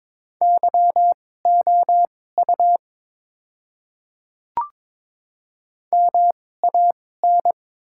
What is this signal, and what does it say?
Morse code, heard as a keyed tone, about 22 wpm, spaced as for 16 wpm, 700 Hz. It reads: YOU MAN
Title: 2x Character Spacing - Top 200 Words - 22wpm